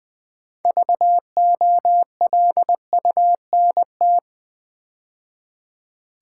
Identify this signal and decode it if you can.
Morse code, heard as a keyed tone, about 20 wpm, 700 Hz. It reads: VOLUNT
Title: Build A Word - 3 Letters At A Time - 9-Letter Words - 20wpm